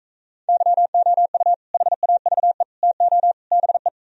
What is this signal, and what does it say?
Morse code, heard as a keyed tone, about 44 wpm, 700 Hz. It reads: YOU HAVE TO BE